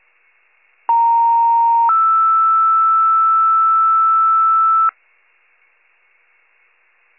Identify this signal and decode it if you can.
Two-tone page, heard as a one-second tone, then a three-second tone, about 930 then 1320 Hz. It is heard from 0.9 to 4.9 s.
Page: 82 FIRE